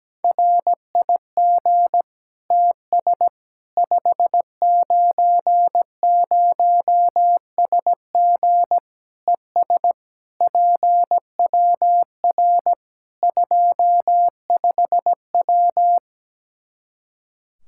Morse code, heard as a keyed tone, about 17 wpm, 700 Hz. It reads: RIG TS 590SG ES PWR 25W
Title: Practice QSO - Exchange 2 - Info - 17wpm